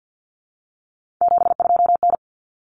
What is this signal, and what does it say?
Morse code, heard as a keyed tone, about 48 wpm, 700 Hz. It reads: 72N
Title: Common Cut Numbers - 3-Digit 50wpm